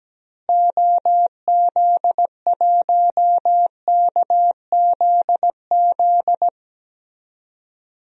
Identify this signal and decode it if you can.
Morse code, heard as a keyed tone, about 17 wpm, 700 Hz. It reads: OZ1KZZ
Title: DXCC-100 Call Signs - 6 Characters - Repeated 2x - 17wpm